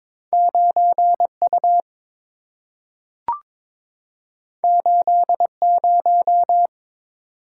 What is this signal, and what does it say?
Morse code, heard as a keyed tone, about 22 wpm, 700 Hz. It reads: 9U 80
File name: practice / 22wpm / Two Letter-Number Combinations - 22wpm